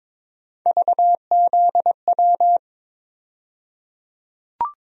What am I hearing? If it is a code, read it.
Morse code, heard as a keyed tone, about 22 wpm, 700 Hz. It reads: VZW
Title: USA Call Sign Postfixes - 3 Letters 22wpm